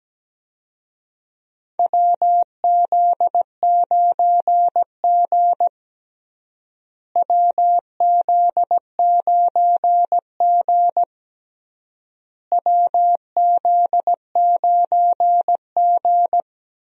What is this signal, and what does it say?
Morse code, heard as a keyed tone, about 17 wpm, 700 Hz. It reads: WZ9G WZ9G WZ9G
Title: USA Call Signs - 4 Characters - Repeated 3x - 17wpm